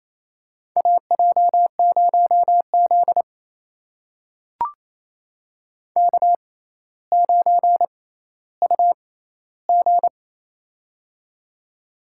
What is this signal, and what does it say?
Morse code, heard as a keyed tone, about 28 wpm, spaced as for 14 wpm, 700 Hz. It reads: AJ0Z K9UG
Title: USA Call Signs - 4 Characters - 6x Character Spacing - 28wpm